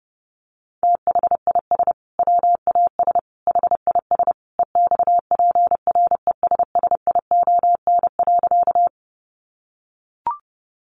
Morse code, heard as a keyed tone, about 30 wpm, 700 Hz. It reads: THIS WAS HIS EXPRESSION.
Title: Sentences from Top 1200 Words 30wpm